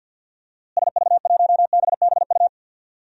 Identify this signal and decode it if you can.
Morse code, heard as a keyed tone, about 52 wpm, 700 Hz. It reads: SV1BBU
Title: DXCC-100 Call Signs - 6 Characters - Repeated 2x - 50wpm